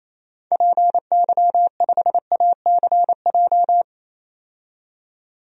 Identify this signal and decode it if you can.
Morse code, heard as a keyed tone, about 28 wpm, 700 Hz. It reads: PY5ACJ
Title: DXCC-100 Call Signs - 6 Characters - 28wpm